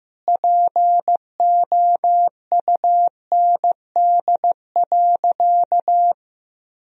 Morse code, heard as a keyed tone, about 16 wpm, 700 Hz. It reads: POUND.
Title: Sentences from Top 1000 Words - Review Part 2 - 15wpm